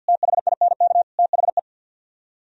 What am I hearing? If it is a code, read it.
Morse code, heard as a keyed tone, about 60 wpm, 700 Hz. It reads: THINK THE
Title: Speed-Racing - Sets of 5 Words 50wpm